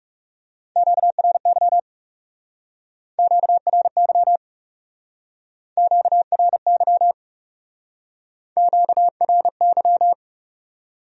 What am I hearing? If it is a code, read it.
Morse code, heard as a keyed tone, about 39 wpm, 700 Hz. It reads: QRY QRY QRY QRY